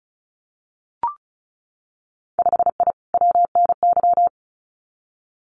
Morse code, heard as a keyed tone, about 36 wpm, 700 Hz. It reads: 5I WNY
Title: Winter Field Day Exchanges Encore 35wpm